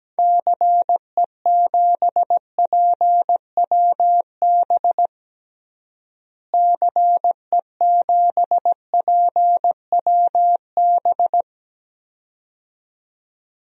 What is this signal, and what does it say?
Morse code, heard as a keyed tone, about 17 wpm, 700 Hz. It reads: CE7PWB CE7PWB
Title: DXCC-100 Call Signs - 6 Characters - Repeated 2x - 17wpm